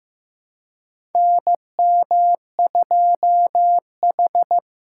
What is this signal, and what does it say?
Morse code, heard as a keyed tone, about 15 wpm, 700 Hz. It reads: NM2H
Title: USA Call Signs - 4 Characters - 2x Character Spacing - 15wpm